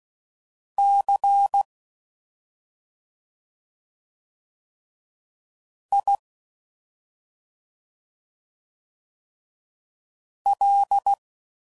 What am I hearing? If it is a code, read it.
Morse code, heard as a keyed tone, about 16 wpm, 800 Hz. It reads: C I L